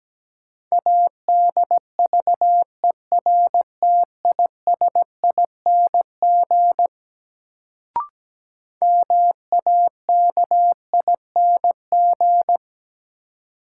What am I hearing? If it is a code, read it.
Morse code, heard as a keyed tone, about 17 wpm, 700 Hz. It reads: ADVERTISING MAKING
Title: Sentences from Top 1000 Words - Review Part 6 - New Word Drill - 17wpm